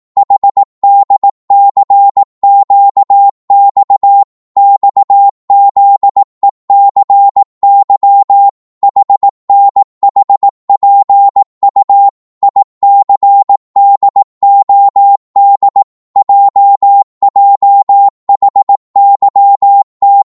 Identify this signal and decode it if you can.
Morse code, heard as a keyed tone, about 18 wpm, 800 Hz. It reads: HDCQX XZECY HNHPU ICDOD JJHYT
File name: Morse / ltr-18wpm